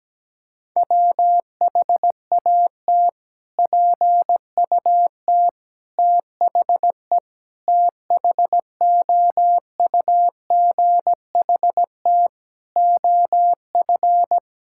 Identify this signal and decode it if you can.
Morse code, heard as a keyed tone, about 17 wpm, 700 Hz. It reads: WHAT PUT THE THOUGHT OF
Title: Sentences from Top 1000 Words - Review Part 5 - 17wpm